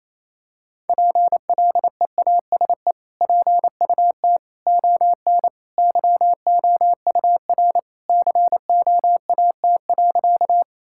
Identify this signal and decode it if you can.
Morse code, heard as a keyed tone, about 28 wpm, 700 Hz. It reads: PLEASE PUT ON YOUR COAT.
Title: Sentences from Top 1600 Words 28wpm